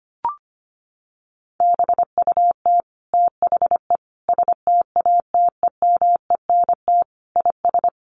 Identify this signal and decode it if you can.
Morse code, heard as a keyed tone, about 25 wpm, 700 Hz. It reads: BUT THE STATEMENT IS
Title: Sentences from Top 1200 Words 25wpm